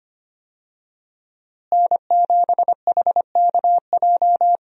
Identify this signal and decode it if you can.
Morse code, heard as a keyed tone, about 25 wpm, 700 Hz. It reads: N7HKJ
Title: USA Call Signs - 5 Characters - Repeated 3x - 25wpm